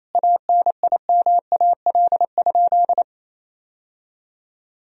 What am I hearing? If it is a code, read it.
Morse code, heard as a keyed tone, about 28 wpm, 700 Hz. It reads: ANIMAL?
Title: Sentences from Top 1000 Words - Review Part 1 - 28wpm